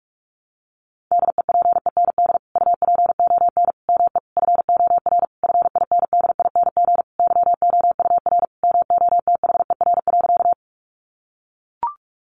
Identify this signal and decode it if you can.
Morse code, heard as a keyed tone, about 44 wpm, 700 Hz. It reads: DEPEND UPON ME FOR FINDING YOUR MOTHER.
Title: Sentences from Top 1300 Words 45wpm